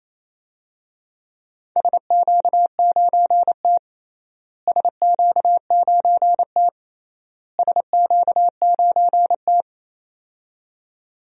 Morse code, heard as a keyed tone, about 28 wpm, 700 Hz. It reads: SQ9T SQ9T SQ9T
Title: DXCC-100 Call Signs - 4 Characters - Repeated 3x - 28wpm